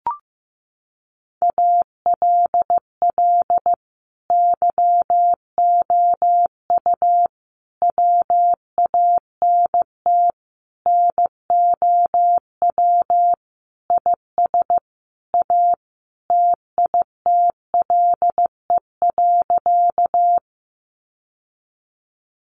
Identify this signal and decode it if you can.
Morse code, heard as a keyed tone, about 15 wpm, 700 Hz. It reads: ALL YOU WANT NOW IS A TITLE.
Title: Sentences from Top 1200 Words 15wpm